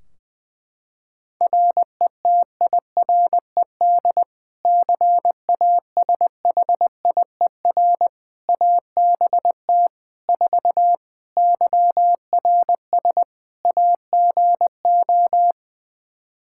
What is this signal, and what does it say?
Morse code, heard as a keyed tone, about 21 wpm, 700 Hz. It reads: RETIRED CASHIER ABT 4 YRS AGO
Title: Practice QSO - Exchange 3 - More Info - 20wpm